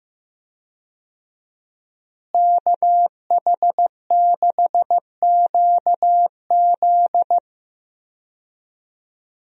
Morse code, heard as a keyed tone, about 15 wpm, 700 Hz. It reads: KH6QZ
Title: USA Call Signs - 5 Characters - Repeated 2x - 15wpm